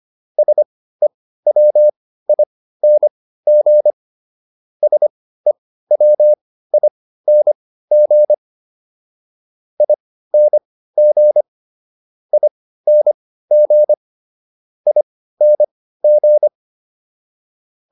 Morse code, heard as a keyed tone, about 25 wpm, spaced as for 15 wpm, 600 Hz. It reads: SEWING SEWING ING ING ING